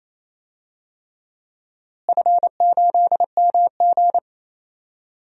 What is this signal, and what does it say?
Morse code, heard as a keyed tone, about 28 wpm, 700 Hz. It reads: F8MG
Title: DXCC-100 Call Signs - 4 Characters - Repeated 3x - 28wpm